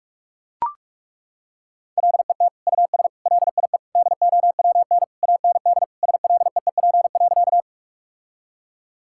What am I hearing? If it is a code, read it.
Morse code, heard as a keyed tone, about 41 wpm, 700 Hz. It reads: LET US LIE DOWN AND SLEEP.
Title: Sentences from Top 1600 Words 45wpm